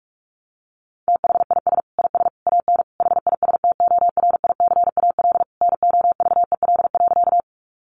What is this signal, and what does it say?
Morse code, heard as a keyed tone, about 46 wpm, 700 Hz. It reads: THIS IS AN HISTORICAL NOVEL.
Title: Sentences from Top 1300 Words 45wpm